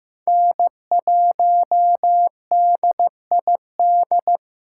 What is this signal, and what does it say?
Morse code, heard as a keyed tone, about 15 wpm, 700 Hz. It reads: N1DID
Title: K1USN SST QSO Practice - 15wpm